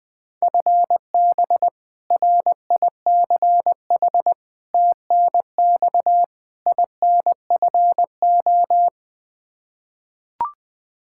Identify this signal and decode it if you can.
Morse code, heard as a keyed tone, about 20 wpm, 700 Hz. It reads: FB RICH TNX INFO
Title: Practice QSO - Exchange 3 - More Info - 20wpm